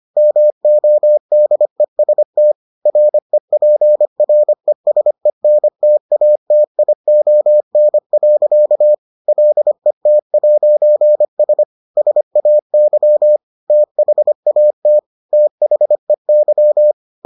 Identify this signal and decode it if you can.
Morse code, heard as a keyed tone, about 25 wpm, 600 Hz. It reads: MODEST REPRESENTATION. LET'S SAY THAT THEY